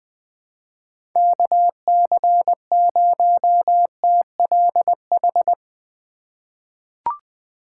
Morse code, heard as a keyed tone, about 20 wpm, 700 Hz. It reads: KC0TLH
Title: USA Call Signs - 6 Characters - 4x Character Spacing - 20wpm